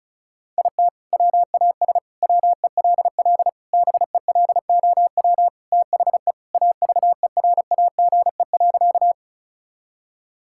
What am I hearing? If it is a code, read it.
Morse code, heard as a keyed tone, about 36 wpm, 700 Hz. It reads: IT WAS WELL BELOW THE AVERAGE.
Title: Sentences from Top 1000 Words - Review Part 6 - 35wpm